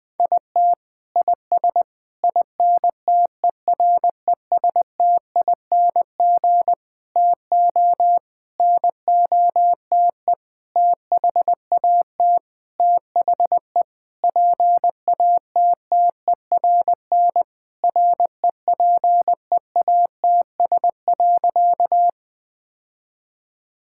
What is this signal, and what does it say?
Morse code, heard as a keyed tone, about 20 wpm, 700 Hz. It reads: IT IS INTERESTING TO NOTE THAT THE PATTERN REPEATS.
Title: Sentences with 6-Word Phrases - 20wpm